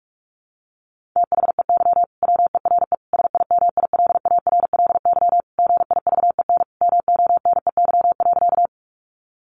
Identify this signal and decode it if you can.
Morse code, heard as a keyed tone, about 44 wpm, 700 Hz. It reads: THEY WERE SIMILARLY GIVEN MONEY.
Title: Sentences from Top 1400 Words 45wpm